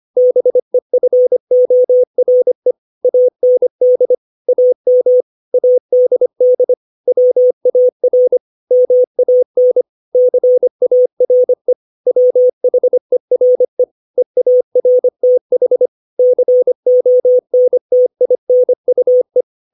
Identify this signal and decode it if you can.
Morse code, heard as a keyed tone, about 25 wpm, 500 Hz. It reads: BEFORE AND AM ADD WAR MAN CARE WHERE EARTH CONTINUE
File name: 632 words / 632 Words 25WPM